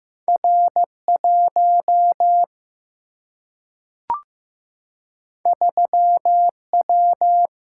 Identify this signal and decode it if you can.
Morse code, heard as a keyed tone, about 15 wpm, 700 Hz. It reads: R1 3W
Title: Single-Letter Single-Number 15wpm